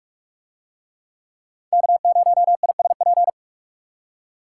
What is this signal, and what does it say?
Morse code, heard as a keyed tone, about 41 wpm, 700 Hz. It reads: K0ISP